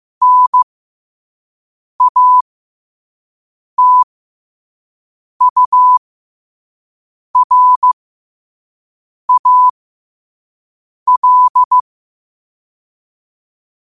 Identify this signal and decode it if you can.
Morse code, heard as a keyed tone, about 15 wpm, spaced as for 5 wpm, 1000 Hz. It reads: NATURAL
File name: qotdfiles / 5wpm